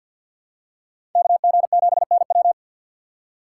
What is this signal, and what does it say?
Morse code, heard as a keyed tone, about 49 wpm, 700 Hz. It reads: KG7NW
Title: USA Call Signs - 5 Characters - 4x Character Spacing - 50wpm